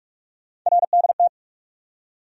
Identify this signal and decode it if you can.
Morse code, heard as a keyed tone, about 42 wpm, 700 Hz. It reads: RDT
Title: USA Call Sign Postfixes - 3 Letters 45wpm